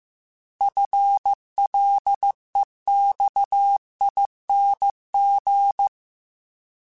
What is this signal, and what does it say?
Morse code, heard as a keyed tone, about 15 wpm, 800 Hz. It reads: FLEXING
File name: FL / FL mp3s / Fast FL @13